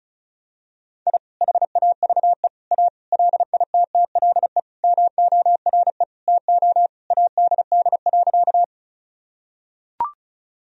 Morse code, heard as a keyed tone, about 36 wpm, 700 Hz. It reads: I HAVE A LITTLE MORE TO ADD.
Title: Sentences from Top 1000 Words - Review Part 1 - 35wpm